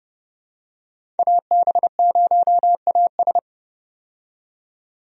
Morse code, heard as a keyed tone, about 29 wpm, 700 Hz. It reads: AB0AS
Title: USA Call Signs - 5 Characters - 30wpm